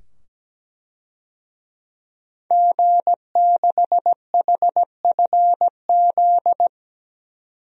Morse code, heard as a keyed tone, about 17 wpm, 700 Hz. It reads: G6HFZ
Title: DXCC-100 Call Signs - 5 Characters - 17wpm